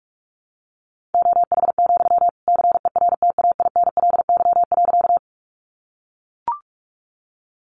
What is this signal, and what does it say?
Morse code, heard as a keyed tone, about 44 wpm, 700 Hz. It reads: OH, CERTAINLY.
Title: Sentences from Top 900 Words 45wpm